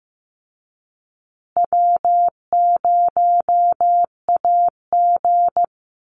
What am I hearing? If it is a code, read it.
Morse code, heard as a keyed tone, about 15 wpm, 700 Hz. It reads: W0AG
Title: Easy Call Signs Repeated 3x - 15wpm